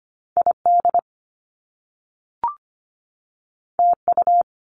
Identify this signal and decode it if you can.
Morse code, heard as a keyed tone, about 25 wpm, 700 Hz. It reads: ID TU
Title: Top Two Letter Combinations in Words 25wpm